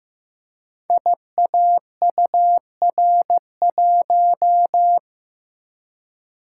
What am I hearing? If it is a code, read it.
Morse code, heard as a keyed tone, about 15 wpm, 700 Hz. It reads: IAUR1
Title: Code Groups - 5 Letter:Numbers - 15wpm